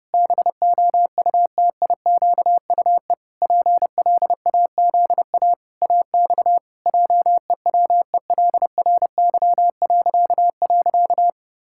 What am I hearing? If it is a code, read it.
Morse code, heard as a keyed tone, about 30 wpm, 700 Hz. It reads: BOUTIQUE PLAZA AX JEWELRY..
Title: Pangram Challenge v2 - 30wpm